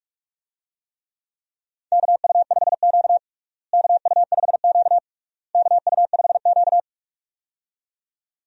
Morse code, heard as a keyed tone, about 46 wpm, 700 Hz. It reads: KU5Q KU5Q KU5Q
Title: USA Call Signs - 4 Characters - Repeated 3x - 45wpm